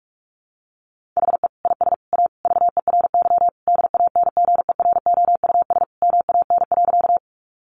Morse code, heard as a keyed tone, about 44 wpm, 700 Hz. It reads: HE IS A VERY DANGEROUS MAN.
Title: Sentences from Top 1200 Words 45wpm